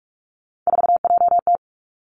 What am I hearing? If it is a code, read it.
Morse code, heard as a keyed tone, about 46 wpm, 700 Hz. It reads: 4JT
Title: Three Challenging Letter-Number Combinations 45wpm